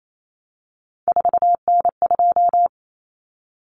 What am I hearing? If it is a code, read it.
Morse code, heard as a keyed tone, about 28 wpm, 700 Hz. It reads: 4N2
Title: Common Cut Numbers - 3-Digit 28wpm